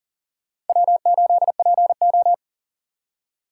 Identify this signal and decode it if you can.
Morse code, heard as a keyed tone, about 54 wpm, spaced as for 47 wpm, 700 Hz. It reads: W8PO
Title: Speed Racing - USA 1x2 Call Signs 40wpm